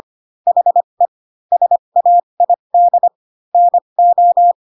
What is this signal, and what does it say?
Morse code, heard as a keyed tone, about 25 wpm, spaced as for 22 wpm, 700 Hz. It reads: HE SAID NO